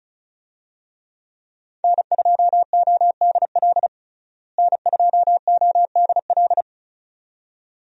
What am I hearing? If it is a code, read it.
Morse code, heard as a keyed tone, about 34 wpm, 700 Hz. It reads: N2ODL N2ODL
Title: USA Call Signs - 5 Characters - Repeated 2x - 35wpm